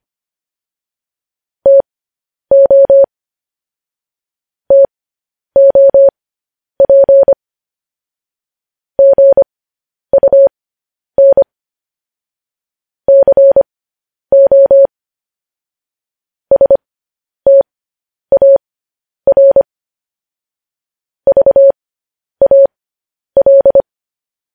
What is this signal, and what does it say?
Morse code, heard as a keyed tone, about 25 wpm, spaced as for 10 wpm, 550 Hz. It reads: TO TOP GUN CO STAR VAL
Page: news headlines in morse code at 10 wpm